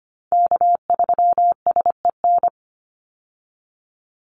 Morse code, heard as a keyed tone, about 25 wpm, 700 Hz. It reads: K3SEN
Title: Call Signs Repeated 3x - 25wpm